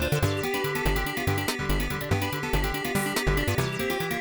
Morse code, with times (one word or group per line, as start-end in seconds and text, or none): none